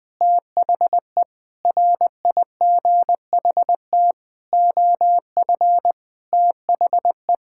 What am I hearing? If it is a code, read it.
Morse code, heard as a keyed tone, about 20 wpm, 700 Hz. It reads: THE RIGHT OF THE
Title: Sentences with 9-Word Phrases - 20wpm